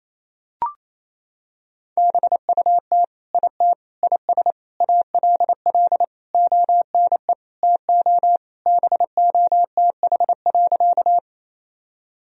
Morse code, heard as a keyed tone, about 27 wpm, 700 Hz. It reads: BUT IT IS ALL ONE TO BOTH.
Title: Sentences from Top 1000 Words - Review Part 1 - 28wpm